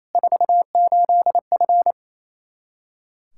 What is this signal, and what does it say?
Morse code, heard as a keyed tone, about 28 wpm, 700 Hz. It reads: 48F
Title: Practice QSO - Exchange 2 - Info - 28wpm